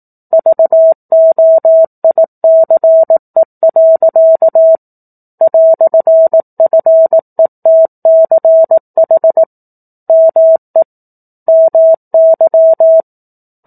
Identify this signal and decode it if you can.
Morse code, heard as a keyed tone, about 18 wpm, 650 Hz. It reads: VOICE. "FETCH ME MY
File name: 04-Chapter-04 18 WPM